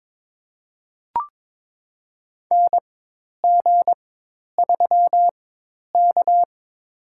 Morse code, heard as a keyed tone, about 22 wpm, spaced as for 10 wpm, 700 Hz. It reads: NG3K
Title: USA Call Signs - 4x Character Spacing - 22wpm